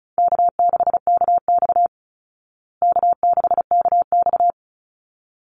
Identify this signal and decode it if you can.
Morse code, heard as a keyed tone, about 34 wpm, 700 Hz. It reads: K6KX K6KX
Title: Easy Call Signs Repeated 2x - 35wpm